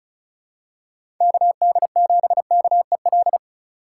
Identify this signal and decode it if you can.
Morse code, heard as a keyed tone, about 36 wpm, 700 Hz. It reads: KD7KEL
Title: USA Call Signs - 6 Characters - 3x Character Spacing - 35wpm